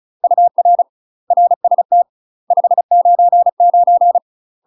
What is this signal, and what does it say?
Morse code, heard as a keyed tone, about 34 wpm, 700 Hz. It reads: UR RST 599
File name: UK MASTER QSOs TRAINING / 35WPM / UK MASTER QSOs 0024